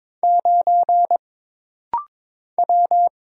Morse code, heard as a keyed tone, about 22 wpm, 700 Hz. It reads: 9 W
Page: This rapid-fire timing is geared to push your ICR (Instant Character Recognition) skills.